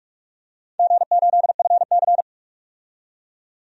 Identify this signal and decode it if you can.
Morse code, heard as a keyed tone, about 46 wpm, 700 Hz. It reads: G8FC
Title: DXCC-100 Call Signs - 4 Characters - 45wpm